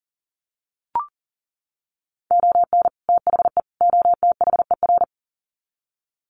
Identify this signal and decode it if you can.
Morse code, heard as a keyed tone, about 40 wpm, 700 Hz. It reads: ON THE OTHER
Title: Sets of 3 Words 40wpm